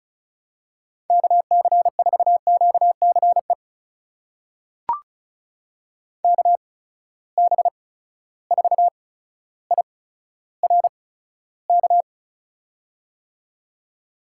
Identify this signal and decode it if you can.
Morse code, heard as a keyed tone, about 36 wpm, spaced as for 15 wpm, 700 Hz. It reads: KC4QCE KB4IRK
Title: USA Call Signs - 6 Characters - 8x Character Spacing - 35wpm